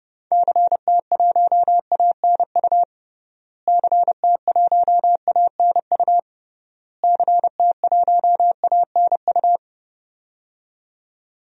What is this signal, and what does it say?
Morse code, heard as a keyed tone, about 30 wpm, 700 Hz. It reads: CT1ANU CT1ANU CT1ANU
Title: DXCC-100 Call Signs - 6 Characters - Repeated 3x - 30wpm